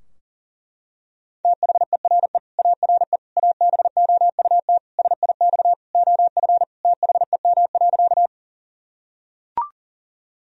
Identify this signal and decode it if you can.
Morse code, heard as a keyed tone, about 40 wpm, 700 Hz. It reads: THERE ARE ABOUT SIX OF THEM.
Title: Sentences from Top 1000 Words - Review Part 2 - 40wpm